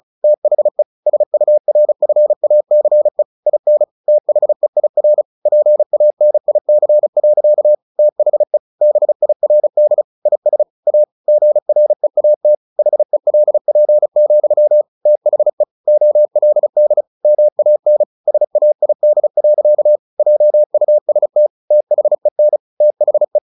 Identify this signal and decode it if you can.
Morse code, heard as a keyed tone, about 36 wpm, 600 Hz. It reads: THE SURFACE IN THEIR PANIC. THE BIRD IS A GREAT HELP, THE OLD MAN SAID. JUST THEN THE
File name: the old man and the sea / the old man and the sea 1 35wpm